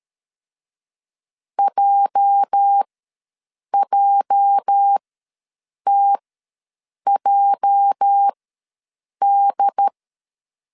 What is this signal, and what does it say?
Morse code, heard as a keyed tone, about 13 wpm, spaced as for 7 wpm, 800 Hz. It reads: JJTJD
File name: LEZIONI / 8 YJ45 / 8 YJ ws 12wpm 3342